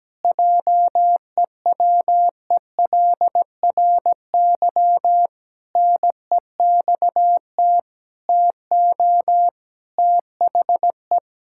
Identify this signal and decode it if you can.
Morse code, heard as a keyed tone, about 17 wpm, 700 Hz. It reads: JEWELRY NEXT TO THE
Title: Pangram Challenge v2 - 17wpm